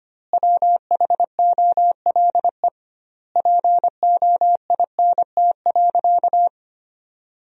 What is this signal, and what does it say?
Morse code, heard as a keyed tone, about 25 wpm, 700 Hz. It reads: WHOLE POINT.
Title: 2x Word-Spacing - Sentences from Top 400 Words - 25wpm